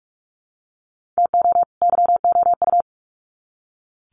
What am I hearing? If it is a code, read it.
Morse code, heard as a keyed tone, about 44 wpm, 700 Hz. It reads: TO YOU
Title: Sets of 2 Words 45wpm